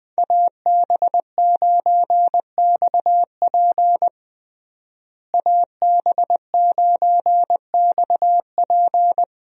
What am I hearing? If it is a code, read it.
Morse code, heard as a keyed tone, about 20 wpm, 700 Hz. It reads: AB9XP AB9XP
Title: USA Call Signs - 5 Characters - Repeated 2x - 20wpm